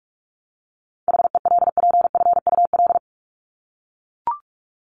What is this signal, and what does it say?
Morse code, heard as a keyed tone, about 44 wpm, 700 Hz. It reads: HELPFUL
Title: Top 751-1000 Words 45wpm